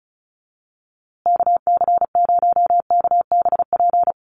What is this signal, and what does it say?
Morse code, heard as a keyed tone, about 34 wpm, 700 Hz. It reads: KC0KBP
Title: USA Call Signs Repeated 3x 35wpm